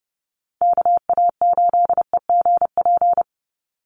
Morse code, heard as a keyed tone, about 30 wpm, 700 Hz. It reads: KA8EGP
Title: USA Call Signs 30wpm